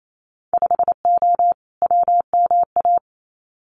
Morse code, heard as a keyed tone, about 28 wpm, 690 Hz. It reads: 5O WMA
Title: Winter Field Day Exchanges Encore 28wpm